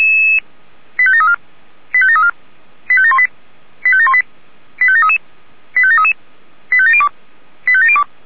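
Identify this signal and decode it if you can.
Five-tone selective call (ZVEI), heard as quick runs of five tones, tones of about 70 ms. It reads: 85734 85734 85718 85718 85733 85733 85702 85702
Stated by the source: Decoded by SDR as ZVEI five-tone selective call.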